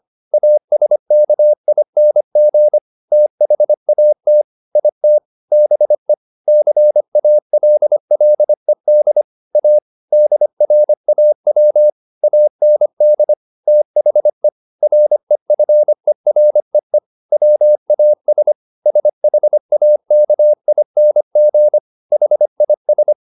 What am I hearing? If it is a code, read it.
Morse code, heard as a keyed tone, about 25 wpm, 600 Hz. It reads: ASKING THAT IT BE CALLED A DRAW AND THE REFEREE WAS SHAKING HIS